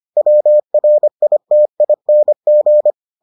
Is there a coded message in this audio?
Yes: Morse code from 0.2 to 2.9 s, heard as a keyed tone, about 25 wpm, 600 Hz. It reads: WRITING